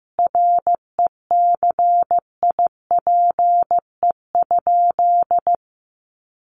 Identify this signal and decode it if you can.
Morse code, heard as a keyed tone, about 15 wpm, 700 Hz. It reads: RECIPE?
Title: Sentences from Top 1200 Words 15wpm